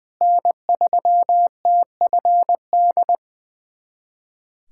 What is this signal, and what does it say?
Morse code, heard as a keyed tone, about 20 wpm, 700 Hz. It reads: N3TFD
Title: USA Callsign DE Callsign - 5 Characters - 20wpm